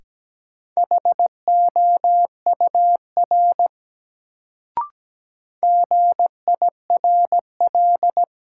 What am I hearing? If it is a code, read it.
Morse code, heard as a keyed tone, about 17 wpm, 700 Hz. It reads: HOUR GIRL
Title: Sentences from Top 400 Words - New Word Drill - 17wpm